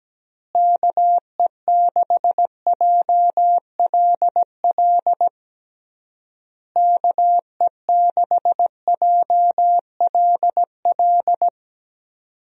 Morse code, heard as a keyed tone, about 17 wpm, 700 Hz. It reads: KE6JLL KE6JLL
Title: USA Call Signs - 6 Characters - Repeated 3x - 17wpm